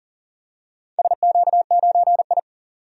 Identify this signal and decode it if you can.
Morse code, heard as a keyed tone, about 38 wpm, 700 Hz. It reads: SQ9I
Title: DXCC-100 Call Signs - 4 Characters - 40wpm